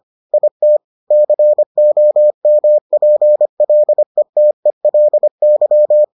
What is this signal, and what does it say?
Morse code, heard as a keyed tone, about 25 wpm, 600 Hz. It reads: IT COMPLETELY